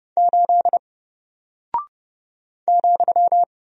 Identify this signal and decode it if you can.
Morse code, heard as a keyed tone, about 30 wpm, 700 Hz. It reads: 8 ,
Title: Single Character (Letter:Number:Punctuation:Prosign) 30wpm